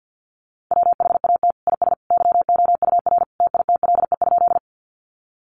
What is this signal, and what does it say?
Morse code, heard as a keyed tone, about 50 wpm, 700 Hz. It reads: WHAT IS YOUR TITLE?
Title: Sentences from Top 1200 Words 50wpm